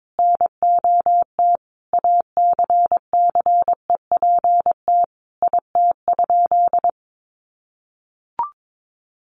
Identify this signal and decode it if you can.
Morse code, heard as a keyed tone, about 22 wpm, 700 Hz. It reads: NOT ACCEPT IT?
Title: Sentences from Top 1300 Words 22wpm